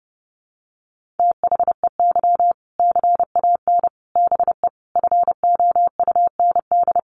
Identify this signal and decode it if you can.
Morse code, heard as a keyed tone, about 30 wpm, 700 Hz. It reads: THEY CAN BE FOUND
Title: Sentences from Top 1500 Words 30wpm